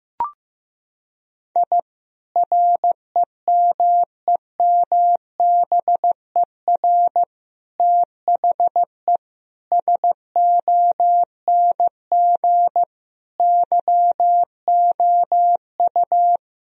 Morse code, heard as a keyed tone, about 15 wpm, 700 Hz. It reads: I REMEMBER THE SONG YOU
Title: Sentences from Top 1000 Words - Review Part 2 - 15wpm